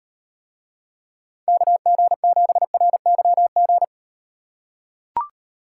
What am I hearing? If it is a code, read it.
Morse code, heard as a keyed tone, about 38 wpm, 700 Hz. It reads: KG7RYG